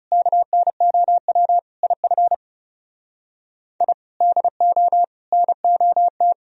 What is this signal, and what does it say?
Morse code, heard as a keyed tone, about 40 wpm, 700 Hz. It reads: KNOW IF I DO NOT